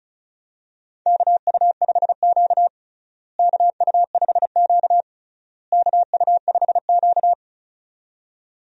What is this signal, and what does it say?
Morse code, heard as a keyed tone, about 36 wpm, 700 Hz. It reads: KU5Q KU5Q KU5Q
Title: USA Call Signs - 4 Characters - Repeated 3x - 35wpm